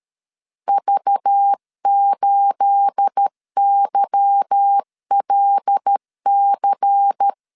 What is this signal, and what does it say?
Morse code, heard as a keyed tone, about 13 wpm, 800 Hz. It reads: V8YLC
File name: LEZIONI / 9 286 12wpm 2305